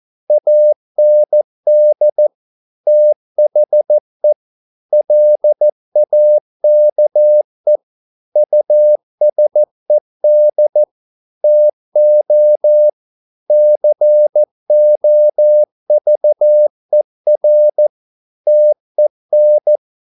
Morse code, heard as a keyed tone, about 14 wpm, 600 Hz. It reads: AND THE LAKE USED TO COVER TEN